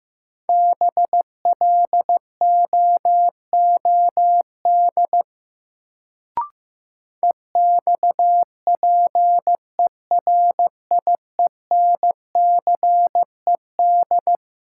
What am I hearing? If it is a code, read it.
Morse code, heard as a keyed tone, about 15 wpm, 700 Hz. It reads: BLOOD EXPERIENCED
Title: Sentences from Top 1100 Words - New Word Drill Alt - 15wpm